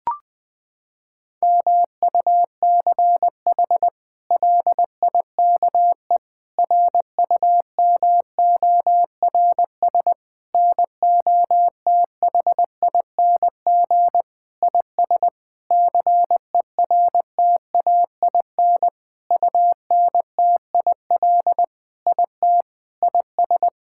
Morse code, heard as a keyed tone, about 20 wpm, 700 Hz. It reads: MUCH LIKE RUMORS NOTHING IS CERTAIN UNTIL IT IS
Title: Sentences with 8-Word Phrases - 20wpm